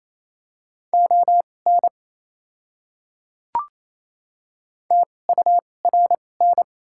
Morse code, heard as a keyed tone, about 28 wpm, spaced as for 20 wpm, 690 Hz. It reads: ON TURN
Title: 2x Character Spacing - Top 300 Words - 28wpm